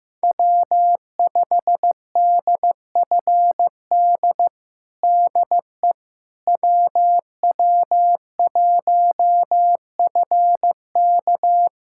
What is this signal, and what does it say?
Morse code, heard as a keyed tone, about 15 wpm, 700 Hz. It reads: W5DFD DE WW1FK
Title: USA Callsign DE Callsign - 5 Characters - 15wpm